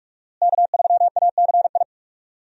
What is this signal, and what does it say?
Morse code, heard as a keyed tone, about 47 wpm, 700 Hz. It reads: K3ACI
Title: USA Call Signs - 5 Characters - Repeated 2x - 45wpm